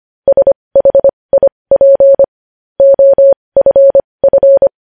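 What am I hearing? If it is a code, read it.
Morse code, heard as a keyed tone, about 25 wpm, spaced as for 20 wpm, 550 Hz. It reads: SHIP OFF
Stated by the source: Morse practice of news headlines at 20 wpm.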